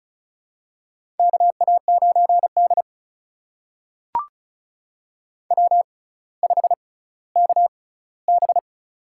Morse code, heard as a keyed tone, about 36 wpm, spaced as for 18 wpm, 700 Hz. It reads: KA9D W5KB
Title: USA Call Signs - 4 Characters - 6x Character Spacing - 35wpm